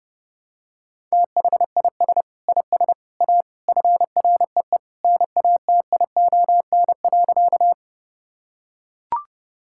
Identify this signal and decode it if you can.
Morse code, heard as a keyed tone, about 32 wpm, 700 Hz. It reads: THIS IS A FREE NATION.
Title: Sentences from Top 1000 Words - Review Part 6 - 30wpm